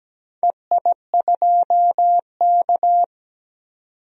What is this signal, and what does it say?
Morse code, heard as a keyed tone, about 17 wpm, 700 Hz. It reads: EI2K
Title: DXCC-100 Call Signs - 4 Characters - Repeated 3x - 17wpm